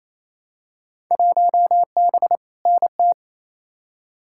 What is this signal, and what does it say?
Morse code, heard as a keyed tone, about 28 wpm, 700 Hz. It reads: 1B NT
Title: Field Day Exchanges Encore - 28wpm